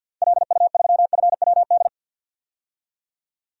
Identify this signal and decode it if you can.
Morse code, heard as a keyed tone, about 50 wpm, 700 Hz. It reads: LU3FWD
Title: DXCC-100 Call Signs - 6 Characters - 50wpm